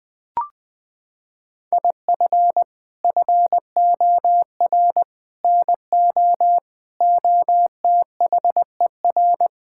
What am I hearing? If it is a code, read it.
Morse code, heard as a keyed tone, about 21 wpm, 700 Hz. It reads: IF FOR NO OTHER
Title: Sets of 6 Words - 20wpm